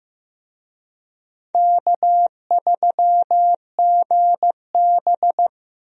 Morse code, heard as a keyed tone, about 15 wpm, 700 Hz. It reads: K3GB
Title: USA Call Signs - 4 Characters - Repeated 3x - 15wpm